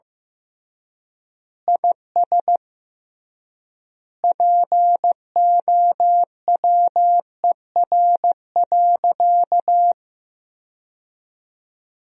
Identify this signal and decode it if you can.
Morse code, heard as a keyed tone, about 15 wpm, 700 Hz. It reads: IS POWER.
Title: 3x Word-Spacing - Sentences from Top 500 Words - 15wpm